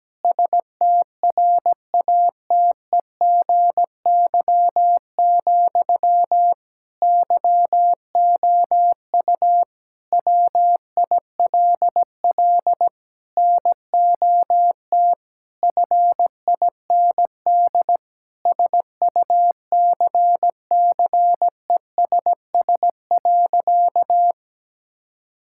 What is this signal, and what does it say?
Morse code, heard as a keyed tone, about 17 wpm, 700 Hz. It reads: STRATEGY, YOU WILL NOT FIND SUCCESS.
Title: Sentences from Top 1000 Words - Review Part 5 - 17wpm